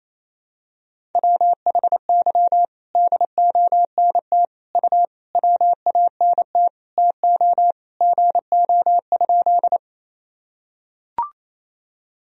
Morse code, heard as a keyed tone, about 28 wpm, 700 Hz. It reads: WHY DONT U WANT TO GO?
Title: Sentences from Top 300 Words and Abbreviations 28wpm